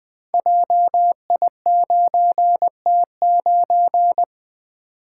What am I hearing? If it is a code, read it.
Morse code, heard as a keyed tone, about 20 wpm, 700 Hz. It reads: JI9T9
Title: Code Groups - 5 Letter:Numbers - 20wpm